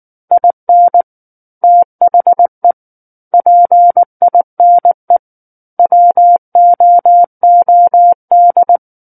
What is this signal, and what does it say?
Morse code, heard as a keyed tone, about 19 wpm, 700 Hz. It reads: IN THE PINE WOOD
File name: Book One / 10-Chapter 10 - 19 WPM